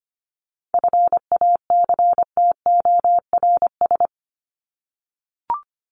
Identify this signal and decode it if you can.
Morse code, heard as a keyed tone, about 25 wpm, 700 Hz. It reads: FACTORS
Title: ES:S Suffix - Top 100 Words with Base Words 25wpm